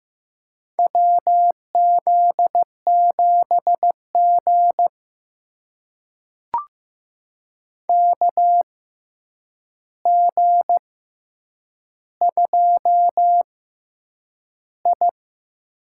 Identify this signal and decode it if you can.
Morse code, heard as a keyed tone, about 15 wpm, spaced as for 8 wpm, 700 Hz. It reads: WZ7G KG2I
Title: USA Call Signs - 4 Characters - 6x Character Spacing - 15wpm